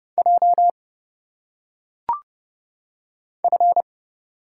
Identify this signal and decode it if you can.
Morse code, heard as a keyed tone, about 30 wpm, 700 Hz. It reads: J F
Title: Single Letters - Phonetic Pronunciation 30wpm